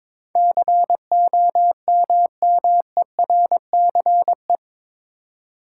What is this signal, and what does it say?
Morse code, heard as a keyed tone, about 21 wpm, 700 Hz. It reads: COMMERCE
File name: practice / 22wpm / Eight Letter Words - 22wpm